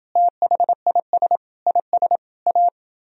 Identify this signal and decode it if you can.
Morse code, heard as a keyed tone, about 29 wpm, 700 Hz. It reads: THIS IS A